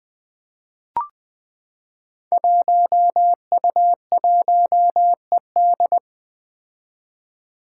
Morse code, heard as a keyed tone, about 21 wpm, 700 Hz. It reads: 1U1ED
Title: Five Challenging Letter-Number Combinations 20wpm